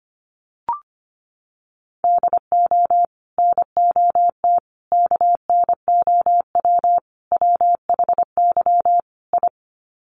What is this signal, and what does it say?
Morse code, heard as a keyed tone, about 25 wpm, 700 Hz. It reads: DO NOT KNOW WHY I
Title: Sets of 5 Words Encore 25wpm